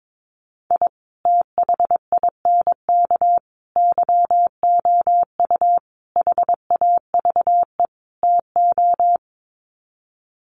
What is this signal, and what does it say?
Morse code, heard as a keyed tone, about 22 wpm, 700 Hz. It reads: I THINK YOU HAVE TO
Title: Sets of 5 Words 22wpm